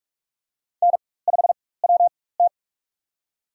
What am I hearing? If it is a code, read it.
Morse code, heard as a keyed tone, about 43 wpm, spaced as for 21 wpm, 700 Hz. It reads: N5WT
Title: USA Call Signs - 4 Characters - 4x Character Spacing - 45wpm